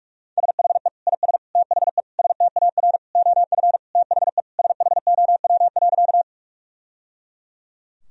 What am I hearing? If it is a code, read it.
Morse code, heard as a keyed tone, about 46 wpm, 700 Hz. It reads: SHE IS THE STAR OF THE SHOW.
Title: Sentences from Top 1000 Words - Review Part 2 - 45wpm